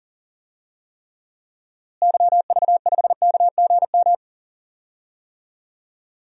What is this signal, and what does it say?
Morse code, heard as a keyed tone, about 40 wpm, 700 Hz. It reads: YV5KGM
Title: DXCC-100 Call Signs - 6 Characters - Repeated 3x - 40wpm